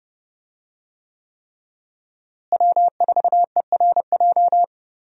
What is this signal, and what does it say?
Morse code, heard as a keyed tone, about 32 wpm, 700 Hz. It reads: W4ERJ
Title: USA Call Signs - 5 Characters - Repeated 3x - 30wpm